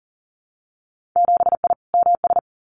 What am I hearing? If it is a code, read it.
Morse code, heard as a keyed tone, about 40 wpm, 700 Hz. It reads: 7I MS
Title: Winter Field Day Exchanges Encore 40wpm